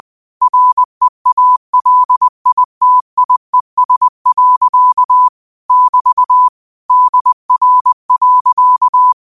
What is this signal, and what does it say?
Morse code, heard as a keyed tone, about 20 wpm, 1000 Hz. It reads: REALITIES. <BT> DR.
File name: qotdfiles / 20wpm